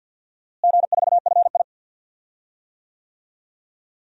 Morse code, heard as a keyed tone, about 53 wpm, 700 Hz. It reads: G4FI
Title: DXCC-100 Call Signs - 4 Characters - Repeated 3x - 50wpm